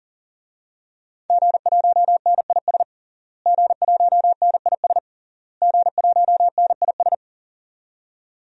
Morse code, heard as a keyed tone, about 40 wpm, 700 Hz. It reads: G1NIS G1NIS G1NIS
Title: DXCC-100 Call Signs - 5 Characters - Repeated 3x - 40wpm